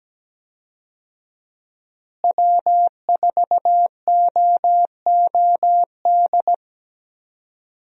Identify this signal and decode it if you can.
Morse code, heard as a keyed tone, about 17 wpm, 700 Hz. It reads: W4OOD
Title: USA Call Signs - 5 Characters - Repeated 2x - 17wpm